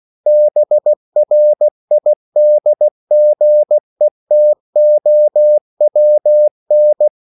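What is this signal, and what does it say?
Morse code, heard as a keyed tone, about 16 wpm, 600 Hz. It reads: BRIDGETOWN
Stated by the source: CW practice of national capitals at 16 words per minute, from ON6ZQ.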